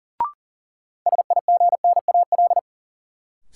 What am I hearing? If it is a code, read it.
Morse code, heard as a keyed tone, about 37 wpm, 700 Hz. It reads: SIGNAL
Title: Sentences from Top 1400 Words - New Word Drill Alt - 40wpm